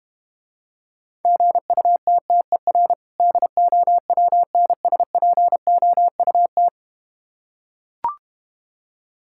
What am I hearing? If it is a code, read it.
Morse code, heard as a keyed tone, about 32 wpm, 700 Hz. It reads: GUTTER DOWNSPOUT